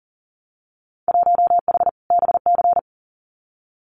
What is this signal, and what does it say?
Morse code, heard as a keyed tone, about 38 wpm, 700 Hz. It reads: 1H BC
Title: Winter Field Day Exchanges 40wpm